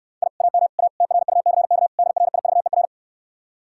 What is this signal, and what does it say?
Morse code, heard as a keyed tone, about 69 wpm, 700 Hz. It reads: I AM A TRUCK DRIVER